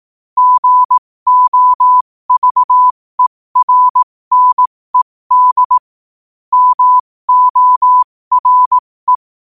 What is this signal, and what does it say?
Morse code, heard as a keyed tone, about 18 wpm, spaced as for 15 wpm, 1000 Hz. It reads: GOVERNED MORE
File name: qotdfiles / 15wpm